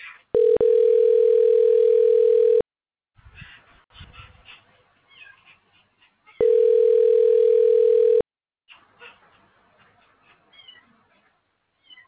Ringback tone (North America), 440 + 480 Hz.